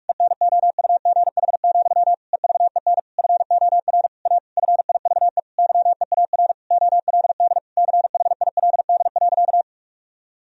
Morse code, heard as a keyed tone, about 41 wpm, 700 Hz. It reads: ENOUGH, EVEN FOR A FIVE YEAR OLD CHILD.
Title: Sentences from Top 1000 Words - Review Part 2 - 45wpm